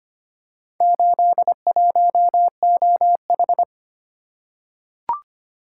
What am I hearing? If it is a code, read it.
Morse code, heard as a keyed tone, about 25 wpm, 700 Hz. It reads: 81OH
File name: practice / 25wpm / Four Letter-Number Combinations - 25wpm